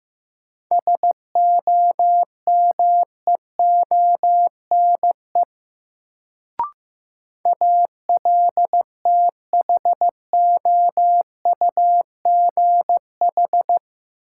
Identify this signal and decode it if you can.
Morse code, heard as a keyed tone, about 15 wpm, 700 Hz. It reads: SOMEONE ALTHOUGH
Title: Sentences from Top 600 Words - New Word Drill - 15wpm